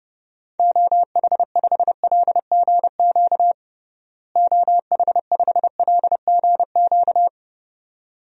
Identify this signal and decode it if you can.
Morse code, heard as a keyed tone, about 32 wpm, 700 Hz. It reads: OH5LGQ OH5LGQ
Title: DXCC-100 Call Signs - 6 Characters - Repeated 2x - 30wpm